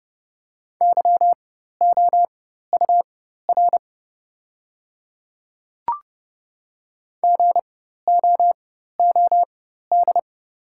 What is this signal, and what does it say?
Morse code, heard as a keyed tone, about 30 wpm, spaced as for 14 wpm, 700 Hz. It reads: YOUR GOOD
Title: 4x Character Spacing - Top 200 Words - 30wpm